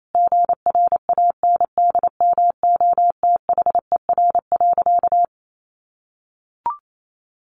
Morse code, heard as a keyed tone, about 28 wpm, 700 Hz. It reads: GRANDMOTHER.
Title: Sentences from Top 1200 Words 28wpm